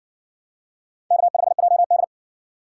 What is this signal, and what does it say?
Morse code, heard as a keyed tone, about 60 wpm, 700 Hz. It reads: K5JD